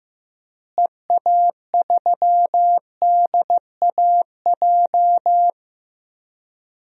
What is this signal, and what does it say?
Morse code, heard as a keyed tone, about 16 wpm, 700 Hz. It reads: EA3DAJ
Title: DXCC-100 Call Signs - 6 Characters - Repeated 3x - 15wpm